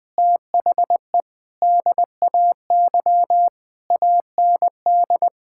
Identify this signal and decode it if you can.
Morse code, heard as a keyed tone, about 20 wpm, 700 Hz. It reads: THE DAY AND
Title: Sets of 8 Words - 20wpm